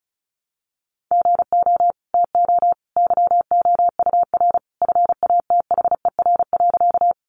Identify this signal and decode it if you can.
Morse code, heard as a keyed tone, about 34 wpm, 700 Hz. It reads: GO TO YOUR FATHER.
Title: Sentences from Top 300 Words 35wpm